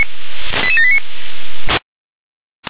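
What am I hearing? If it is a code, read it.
Five-tone selective call (ZVEI), heard as quick runs of five tones, tones of about 70 ms. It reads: 88789
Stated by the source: Received on an SDR.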